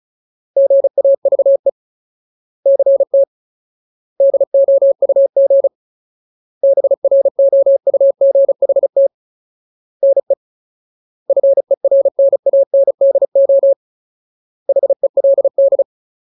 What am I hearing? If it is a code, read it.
Morse code, heard as a keyed tone, about 34 wpm, 550 Hz. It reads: GAVE CT DOUG BROUGHT NE FERNANDO HELD
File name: Ultimate Word List / 35WPM Ultimate